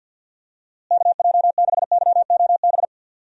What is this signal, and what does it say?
Morse code, heard as a keyed tone, about 48 wpm, 700 Hz. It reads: KJ6YOB
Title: USA Call Signs - 6 Characters - Repeated 2x - 50wpm